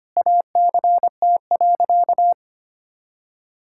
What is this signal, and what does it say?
Morse code, heard as a keyed tone, about 25 wpm, 700 Hz. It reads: ACT.
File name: practice / 25wpm / Difficult Sentences with Hints - 25wpm